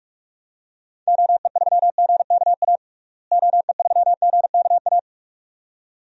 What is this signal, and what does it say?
Morse code, heard as a keyed tone, about 46 wpm, 700 Hz. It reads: OE3GKA OE3GKA
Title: DXCC-100 Call Signs - 6 Characters - Repeated 2x - 45wpm